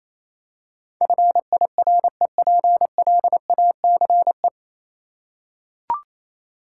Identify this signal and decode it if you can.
Morse code, heard as a keyed tone, about 27 wpm, 700 Hz. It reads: FIREPLACE
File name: Easy Words - Speed-Racing with Hints - 22wpm